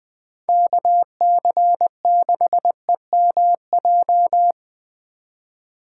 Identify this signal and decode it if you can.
Morse code, heard as a keyed tone, about 19 wpm, 700 Hz. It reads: KC6EMJ
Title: Speed Racing - USA 2x3 Call Signs 17wpm